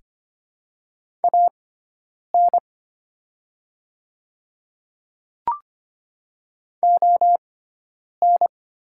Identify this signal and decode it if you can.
Morse code, heard as a keyed tone, about 25 wpm, spaced as for 9 wpm, 700 Hz. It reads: AN ON
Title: 6x Character Spacing - Top 200 Words - 25wpm